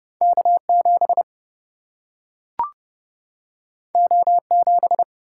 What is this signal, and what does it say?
Morse code, heard as a keyed tone, about 30 wpm, 700 Hz. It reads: K7 O7
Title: Single-Letter Single-Number 30wpm